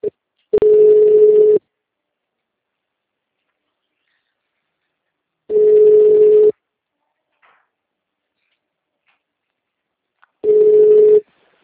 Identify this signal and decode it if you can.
Ringback tone (Europe, 425 Hz).